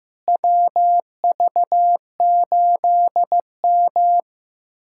Morse code, heard as a keyed tone, about 16 wpm, 700 Hz. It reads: WV8M
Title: USA Callsign DE Callsign - 4 Characters - 15wpm